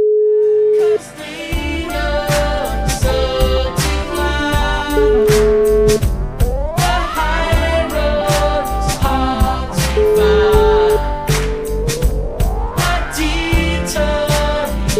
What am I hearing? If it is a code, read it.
Ringback tone (Europe, 425 Hz), with noise.